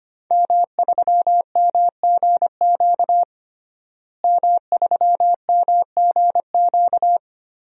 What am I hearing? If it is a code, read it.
Morse code, heard as a keyed tone, about 25 wpm, 700 Hz. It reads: M3MGQ M3MGQ
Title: DXCC-100 Call Signs - 5 Characters - Repeated 3x - 25wpm